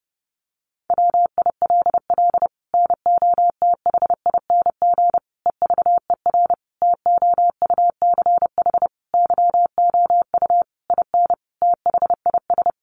Morse code, heard as a keyed tone, about 30 wpm, 700 Hz. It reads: WILL NOTHING EVER TOUCH YOU IN THIS
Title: Sentences from Top 1200 Words 30wpm